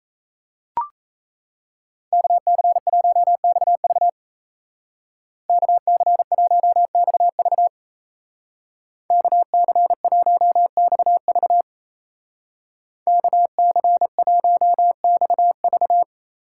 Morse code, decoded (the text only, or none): KC1XV KC1XV KC1XV KC1XV